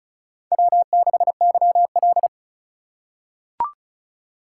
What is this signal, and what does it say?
Morse code, heard as a keyed tone, about 36 wpm, 700 Hz. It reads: W6YL
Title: USA Call Signs - 2x Character Spacing - 35wpm